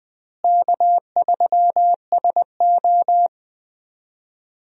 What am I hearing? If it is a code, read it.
Morse code, heard as a keyed tone, about 20 wpm, 700 Hz. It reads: K3SO
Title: USA Call Signs - 4 Characters - Repeated 3x - 20wpm